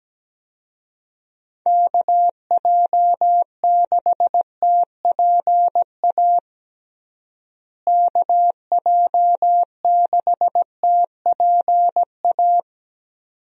Morse code, heard as a keyed tone, about 17 wpm, 700 Hz. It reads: KJ6TPA KJ6TPA
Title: USA Call Signs - 6 Characters - Repeated 2x - 17wpm